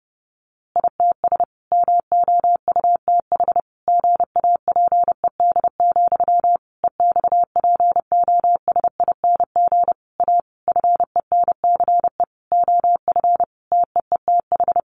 Morse code, heard as a keyed tone, about 30 wpm, 700 Hz. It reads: ITS MOUTH GAPED, EXPOSING A FENCE OF TEETH
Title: A Sound Of Thunder 30wpm